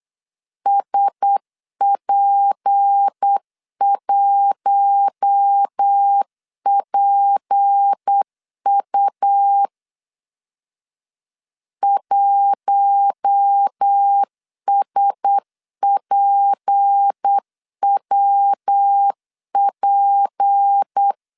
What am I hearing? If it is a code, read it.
Morse code, heard as a keyed tone, about 8 wpm, 800 Hz. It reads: SP1PU 1SPWP
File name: LEZIONI / 6 PKX0 / 6 P 8wpm 3025